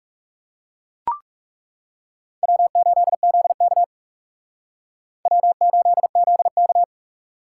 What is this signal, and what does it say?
Morse code, heard as a keyed tone, about 39 wpm, 700 Hz. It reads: W8ZK W8ZK